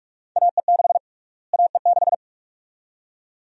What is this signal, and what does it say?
Morse code, heard as a keyed tone, about 42 wpm, 700 Hz. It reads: AE6 AE6
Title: USA Call Sign Prefixes - Repeated 2x - 45wpm